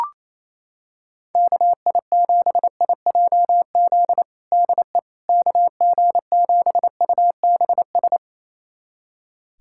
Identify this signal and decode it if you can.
Morse code, heard as a keyed tone, about 28 wpm, 700 Hz. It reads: KI7IJZ DE KG7UBS
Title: USA Callsign DE Callsign 28wpm